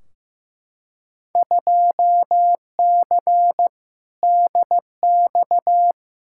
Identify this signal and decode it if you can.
Morse code, heard as a keyed tone, about 16 wpm, 700 Hz. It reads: 2C DX
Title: Field Day Exchanges Encore - 15wpm